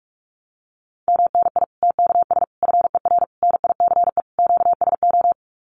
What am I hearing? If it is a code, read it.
Morse code, heard as a keyed tone, about 44 wpm, 700 Hz. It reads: MNI TKS FER NICE QSO
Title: Common QSO Phrases 45wpm